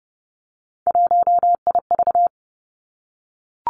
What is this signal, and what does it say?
Morse code, heard as a keyed tone, about 30 wpm, 700 Hz. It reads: 1IV
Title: Three Challenging Letter-Number Combinations 30wpm